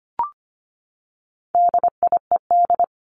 Morse code, heard as a keyed tone, about 25 wpm, 700 Hz. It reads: DIED
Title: ED Suffix - Top 100 Words with Base Words 25wpm